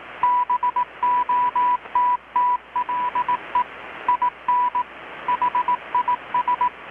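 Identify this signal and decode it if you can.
Morse code, heard as a keyed tone, about 18 wpm, 990 Hz. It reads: BOTTLE IN HIS
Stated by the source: Morse from KPH's Night of Nights XX maritime radiotelegraphy event, received in Chiapas, Mexico about 3,800 km away on a Kenwood R-600 with a grounded 10 m long-wire antenna.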